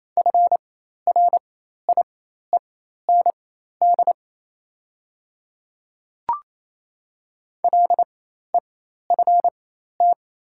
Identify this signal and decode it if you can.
Morse code, heard as a keyed tone, about 28 wpm, spaced as for 13 wpm, 700 Hz. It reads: FRIEND LEFT
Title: 4x Character Spacing - Top 300 Words - 28wpm